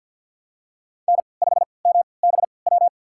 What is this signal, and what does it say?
Morse code, heard as a keyed tone, about 52 wpm, spaced as for 28 wpm, 700 Hz. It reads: N5MBW
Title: USA Call Signs - 5 Characters - 3x Character Spacing - 50wpm